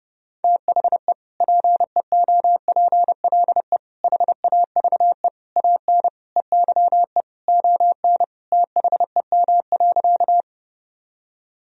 Morse code, heard as a keyed tone, about 30 wpm, 700 Hz. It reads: THE PEOPLE HAVE AN EYE ON THEM.
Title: Sentences from Top 1000 Words - Review Part 1 - 30wpm